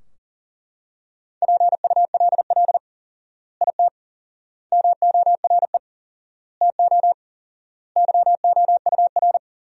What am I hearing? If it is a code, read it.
Morse code, heard as a keyed tone, about 40 wpm, 700 Hz. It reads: PULL IT MORE TO YOUR
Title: 4x Word-Spacing - Sentences from Top 400 Words - 40wpm